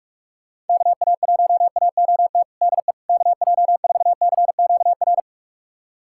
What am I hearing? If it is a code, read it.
Morse code, heard as a keyed tone, about 42 wpm, 700 Hz. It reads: KA1AOT DE KJ4CQR
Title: USA Callsign DE Callsign 45wpm